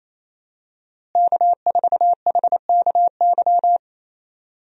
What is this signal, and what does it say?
Morse code, heard as a keyed tone, about 28 wpm, 700 Hz. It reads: K4HKY
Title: USA Call Signs - 5 Characters - Repeated 2x - 28wpm